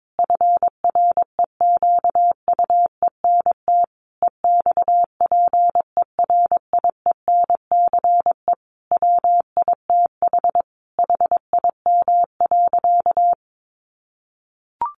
Morse code, heard as a keyed tone, about 22 wpm, 700 Hz. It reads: FREQUENT EXPERIENCE WITH HIM.
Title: Sentences from Top 1400 Words 22wpm